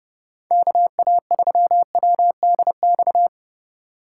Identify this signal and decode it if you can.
Morse code, heard as a keyed tone, about 40 wpm, 700 Hz. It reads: KA3WDX